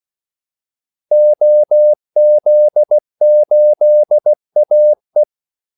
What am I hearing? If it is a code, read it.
Morse code, heard as a keyed tone, about 16 wpm, 600 Hz. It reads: OZ8AE
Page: Kaldesignaler 16 wpm